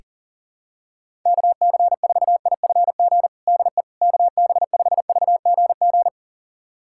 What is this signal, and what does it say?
Morse code, heard as a keyed tone, about 40 wpm, 700 Hz. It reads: KC4IFG DE KB5VGG
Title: USA Callsign DE Callsign - 6 Characters - 40wpm